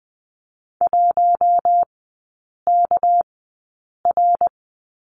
Morse code, heard as a keyed tone, about 20 wpm, spaced as for 8 wpm, 700 Hz. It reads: 1KR
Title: 198 Lesson 32e - Letter k - Three Characters 20wpm